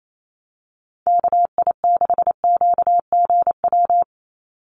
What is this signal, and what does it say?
Morse code, heard as a keyed tone, about 28 wpm, 700 Hz. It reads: KI6QGW
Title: USA Call Signs 28wpm